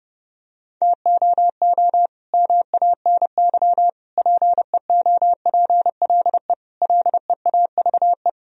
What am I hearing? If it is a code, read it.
Morse code, heard as a keyed tone, about 30 wpm, 700 Hz. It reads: TOO MANY PEOPLE LEAVE
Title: Sentences from Top 1000 Words - Review Part 1 - 30wpm